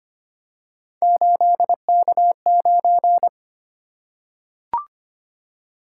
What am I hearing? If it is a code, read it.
Morse code, heard as a keyed tone, about 25 wpm, 700 Hz. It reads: 8K9
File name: Three Letter-Number Combinations - 25wpm